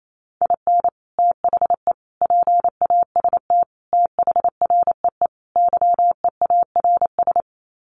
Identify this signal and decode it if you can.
Morse code, heard as a keyed tone, about 28 wpm, 700 Hz. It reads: IN THE PAST THREE YEARS
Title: Sets of 5 Words Encore2 28wpm